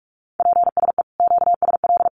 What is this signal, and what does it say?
Morse code, heard as a keyed tone, about 44 wpm, 700 Hz. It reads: PSE QSL
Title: Common QSO Phrases 45wpm